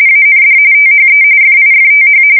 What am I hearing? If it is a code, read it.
RTTY, about 45.45 baud. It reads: AND THE FIFTH
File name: fsk170 / USS-New-Jersey